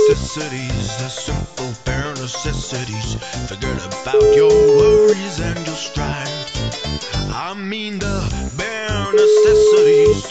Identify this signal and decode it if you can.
Ringback tone (Europe, 425 Hz), with speech.